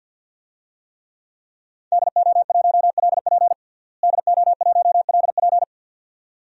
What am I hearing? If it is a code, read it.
Morse code, heard as a keyed tone, about 48 wpm, 700 Hz. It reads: DO1LP DO1LP
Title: DXCC-100 Call Signs - 5 Characters - Repeated 2x - 50wpm